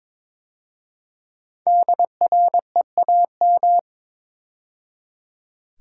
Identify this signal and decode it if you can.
Morse code, heard as a keyed tone, about 22 wpm, 700 Hz. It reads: DREAM
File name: 5 Letter Words - First 3 Letters Given - 22wpm